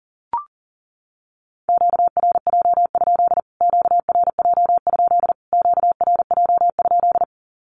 Morse code, heard as a keyed tone, about 40 wpm, 700 Hz. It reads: QRJ? QRJ? QRJ?
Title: Q-Codes Repeated 3x 40wpm